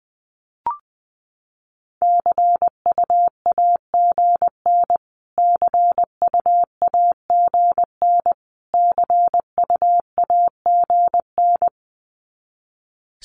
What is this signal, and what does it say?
Morse code, heard as a keyed tone, about 20 wpm, 700 Hz. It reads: CUAGN CUAGN CUAGN
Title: Abbreviations Repeated 3x - 20wpm